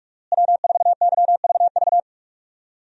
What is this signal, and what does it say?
Morse code, heard as a keyed tone, about 46 wpm, 700 Hz. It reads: W4YVV
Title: USA Call Signs - 5 Characters - Repeated 2x - 45wpm